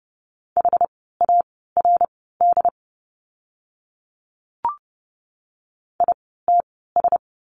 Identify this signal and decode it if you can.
Morse code, heard as a keyed tone, about 30 wpm, spaced as for 17 wpm, 700 Hz. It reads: HARD ITS
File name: Lesson 17f - Letter u - Words 30wpm